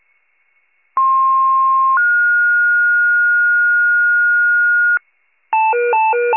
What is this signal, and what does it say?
Two-tone page, heard as a one-second tone, then a three-second tone, about 1060 then 1430 Hz, with noise. It is heard from 1.0 to 5.0 s.
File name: pager / 98 FIRE / wcalert